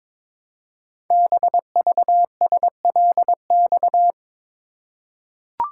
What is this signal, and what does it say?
Morse code, heard as a keyed tone, about 22 wpm, 700 Hz. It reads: BVSLX
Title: Code Groups - 5 Letter:Numbers - 22wpm